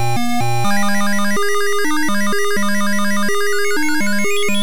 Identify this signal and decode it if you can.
There is no code.